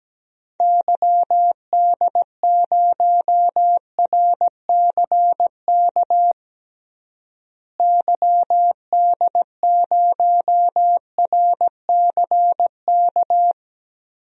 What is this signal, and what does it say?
Morse code, heard as a keyed tone, about 17 wpm, 700 Hz. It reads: YD0RCK YD0RCK
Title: DXCC-100 Call Signs - 6 Characters - Repeated 3x - 17wpm